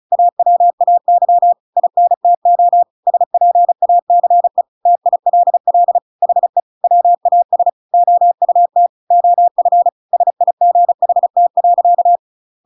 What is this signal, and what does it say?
Morse code, heard as a keyed tone, about 34 wpm, 700 Hz. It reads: AWAY INTO SPACE TILL HE WAS OUT OF SIGHT.